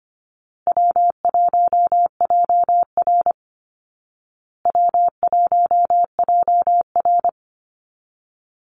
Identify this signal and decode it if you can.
Morse code, heard as a keyed tone, about 25 wpm, 700 Hz. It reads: W1JR W1JR
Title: Easy Call Signs Repeated 3x - 25wpm